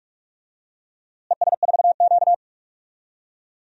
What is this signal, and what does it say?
Morse code, heard as a keyed tone, about 47 wpm, 700 Hz. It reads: ES4Q
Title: DXCC-100 Call Signs - 4 Characters - Repeated 2x - 45wpm